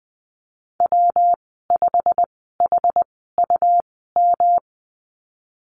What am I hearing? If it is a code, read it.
Morse code, heard as a keyed tone, about 20 wpm, spaced as for 15 wpm, 700 Hz. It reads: W5HUM
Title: 179 Lesson 29g - Slash - Call Signs 20wpm